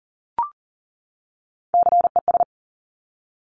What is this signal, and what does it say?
Morse code, heard as a keyed tone, about 39 wpm, 700 Hz. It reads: CES
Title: Top Three Letter Combinations in Words 40wpm